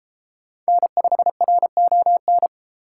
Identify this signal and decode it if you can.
Morse code, heard as a keyed tone, about 32 wpm, 700 Hz. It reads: N5RON